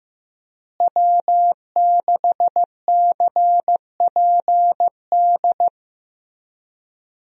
Morse code, heard as a keyed tone, about 15 wpm, 700 Hz. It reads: W6CPD
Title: USA Call Signs - 5 Characters - Repeated 3x - 15wpm